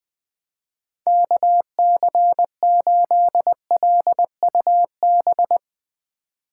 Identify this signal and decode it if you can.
Morse code, heard as a keyed tone, about 20 wpm, 700 Hz. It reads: KC8LUB
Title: 2023 POTA Hunter Call Signs - 20wpm